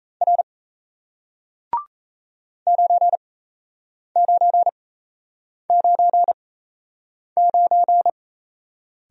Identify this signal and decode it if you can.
Morse code, heard as a keyed tone, about 38 wpm, spaced as for 8 wpm, 700 Hz. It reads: R 9999